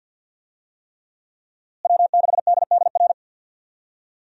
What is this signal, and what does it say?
Morse code, heard as a keyed tone, about 46 wpm, 700 Hz. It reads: W6DDR
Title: USA Call Signs - 5 Characters - Repeated 3x - 50wpm